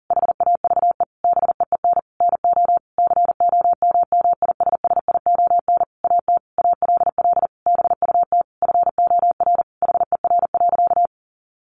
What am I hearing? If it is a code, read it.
Morse code, heard as a keyed tone, about 43 wpm, 700 Hz. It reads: HAVE BEEN NO COMMISSION AT ALL BUT FOR HER.
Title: Sentences from Top 1400 Words 40wpm